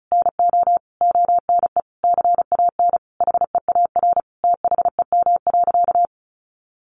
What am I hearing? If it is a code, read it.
Morse code, heard as a keyed tone, about 34 wpm, 700 Hz. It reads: NO ONE CAN HEAR THEM.
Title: Sentences from Top 600 Words 35wpm